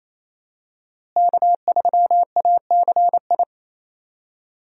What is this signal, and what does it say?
Morse code, heard as a keyed tone, about 27 wpm, 700 Hz. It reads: K3ACI
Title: USA Call Signs - 5 Characters - Repeated 2x - 28wpm